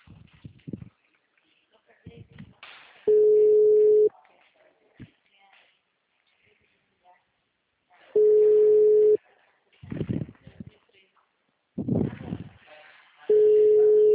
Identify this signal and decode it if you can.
Ringback tone (Europe, 425 Hz), with speech.
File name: out- -2020- - - .84